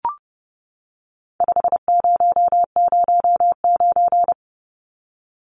Morse code, heard as a keyed tone, about 30 wpm, 700 Hz. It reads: 5009
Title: Numbers - 4-Digit 30wpm